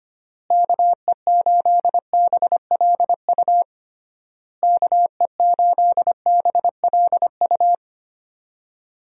Morse code, heard as a keyed tone, about 25 wpm, 700 Hz. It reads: KE8BLU KE8BLU
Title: USA Call Signs - 6 Characters - Repeated 3x - 25wpm